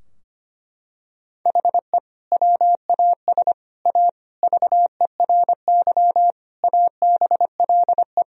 Morse code, heard as a keyed tone, about 25 wpm, 700 Hz. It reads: HE WAS A VERY ABLE
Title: Sentences from Top 1000 Words - Review Part 2 - 25wpm